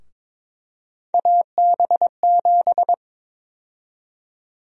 Morse code, heard as a keyed tone, about 21 wpm, 700 Hz. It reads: AB7
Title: USA Call Sign Prefixes - Repeated 2x - 22wpm